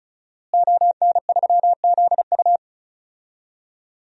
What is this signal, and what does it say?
Morse code, heard as a keyed tone, about 34 wpm, 700 Hz. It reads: ON3ZU
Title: DXCC-100 Call Signs - 5 Characters - Repeated 3x - 35wpm